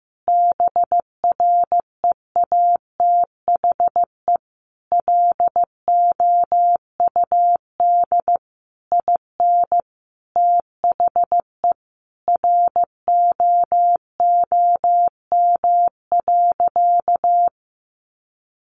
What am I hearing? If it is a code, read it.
Morse code, heard as a keyed tone, about 15 wpm, 700 Hz. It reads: BREATHE LOUD IN THE ROOM.
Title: A Sound Of Thunder 15wpm